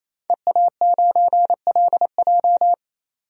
Morse code, heard as a keyed tone, about 28 wpm, 700 Hz. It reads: EA9LJ
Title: DXCC-100 Call Signs - 5 Characters - 28wpm